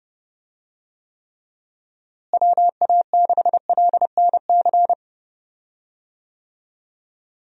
Morse code, heard as a keyed tone, about 30 wpm, 700 Hz. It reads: WA6LNC
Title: USA Call Signs - 6 Characters - Repeated 3x - 30wpm